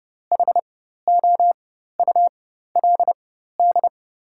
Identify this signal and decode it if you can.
Morse code, heard as a keyed tone, about 30 wpm, spaced as for 14 wpm, 700 Hz. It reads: HOULD
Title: 4x Character Spacing - Top 500 Words - 30wpm